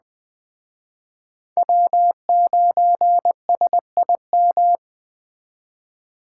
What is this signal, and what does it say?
Morse code, heard as a keyed tone, about 21 wpm, 700 Hz. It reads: W9SIM
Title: Speed Racing - USA 1x3 Call Signs 17wpm